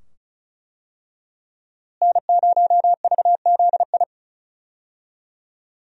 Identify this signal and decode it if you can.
Morse code, heard as a keyed tone, about 36 wpm, 700 Hz. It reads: N0VZI
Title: USA Call Signs - 5 Characters - Repeated 2x - 35wpm